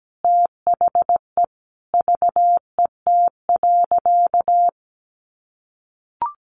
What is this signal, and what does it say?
Morse code, heard as a keyed tone, about 17 wpm, 700 Hz. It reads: THE VET.
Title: Easy Phrases 17wpm